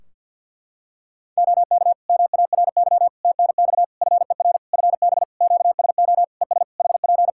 Morse code, heard as a keyed tone, about 48 wpm, 700 Hz. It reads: OK MARY TNX FER FB QSO ES HP
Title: Practice QSO - Exchange 4 - Ending - 50wpm